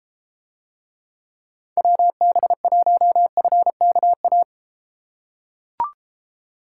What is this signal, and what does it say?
Morse code, heard as a keyed tone, about 33 wpm, 700 Hz. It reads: WB1FKA